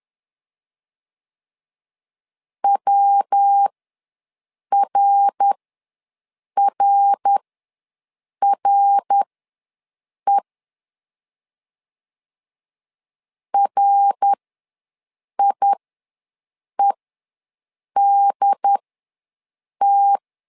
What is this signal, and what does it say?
Morse code, heard as a keyed tone, about 10 wpm, spaced as for 6 wpm, 800 Hz. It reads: WRRRE RIEDT
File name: LEZIONI / 3 MWRO / 3 MWR ws 10wpm 2151